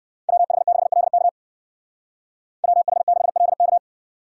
Morse code, heard as a keyed tone, about 55 wpm, 690 Hz. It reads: WH6LK WH6LK
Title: Speed Racing - USA 2x2 Call Signs 45wpm